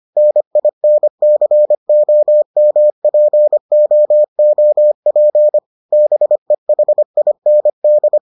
Morse code, heard as a keyed tone, about 25 wpm, 600 Hz. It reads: NINCOMPOOP BEHIND